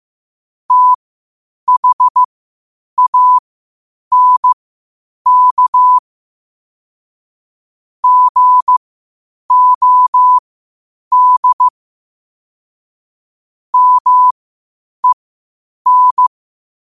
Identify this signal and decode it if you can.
Morse code, heard as a keyed tone, about 15 wpm, spaced as for 8 wpm, 1000 Hz. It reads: THANK GOD MEN
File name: qotdfiles / 8wpm